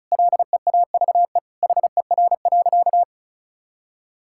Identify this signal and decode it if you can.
Morse code, heard as a keyed tone, about 36 wpm, 700 Hz. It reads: LEAVE HER.
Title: Sentences from Top 1000 Words - Review Part 5 - 35wpm